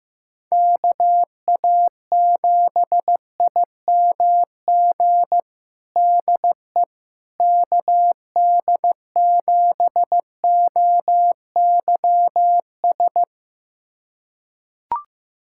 Morse code, heard as a keyed tone, about 16 wpm, 700 Hz. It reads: KA7IMG DE KD7OYS
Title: USA Callsign DE Callsign - 6 Characters - 15wpm